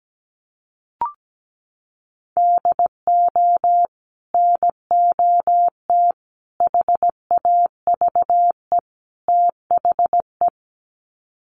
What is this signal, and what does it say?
Morse code, heard as a keyed tone, about 17 wpm, 700 Hz. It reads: DO NOT HAVE THE
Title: Sets of 4 Words 17wpm